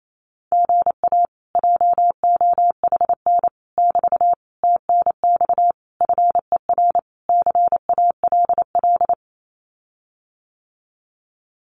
Morse code, heard as a keyed tone, about 28 wpm, 700 Hz. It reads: GA JOHN <BT> TNX FER CALL
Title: Common QSO Phrases Encore 28wpm